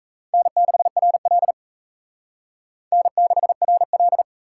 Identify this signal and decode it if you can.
Morse code, heard as a keyed tone, about 35 wpm, 700 Hz. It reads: N6RL N6RL
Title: Speed Racing - USA Call Signs 28wpm